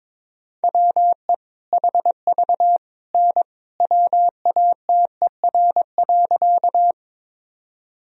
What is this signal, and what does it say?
Morse code, heard as a keyed tone, about 22 wpm, 700 Hz. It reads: WE HV N WATER.
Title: Sentences from Top 100 Words and Abbreviations 22wpm